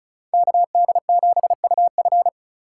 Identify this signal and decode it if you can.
Morse code, heard as a keyed tone, about 34 wpm, 700 Hz. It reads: KD7UF
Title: USA Call Signs - 4x Character Spacing - 35wpm